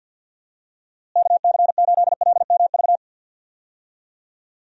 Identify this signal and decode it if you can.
Morse code, heard as a keyed tone, about 51 wpm, 700 Hz. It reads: KC8LMV
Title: USA Call Signs - 6 Characters - Repeated 2x - 50wpm